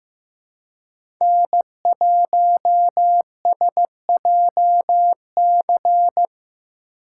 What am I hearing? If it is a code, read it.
Morse code, heard as a keyed tone, about 15 wpm, 700 Hz. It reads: N1SJC
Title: USA Call Signs - 2x Character Spacing - 15wpm